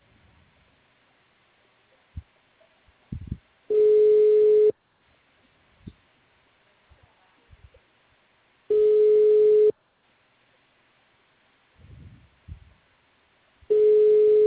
Ringback tone (Europe, 425 Hz).